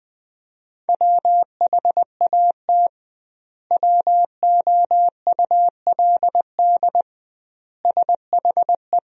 Morse code, heard as a keyed tone, about 21 wpm, 700 Hz. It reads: WHAT WOULD SHE
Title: 2x Word-Spacing - Sentences from Top 100 Words - 20wpm